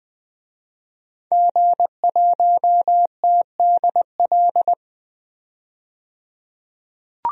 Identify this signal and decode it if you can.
Morse code, heard as a keyed tone, about 20 wpm, 700 Hz. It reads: G1TDL
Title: DXCC-100 Call Signs - 5 Characters - 20wpm